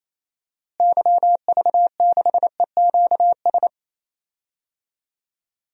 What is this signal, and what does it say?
Morse code, heard as a keyed tone, about 28 wpm, 700 Hz. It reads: YV6EQS
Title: DXCC-100 Call Signs - 6 Characters - Repeated 3x - 28wpm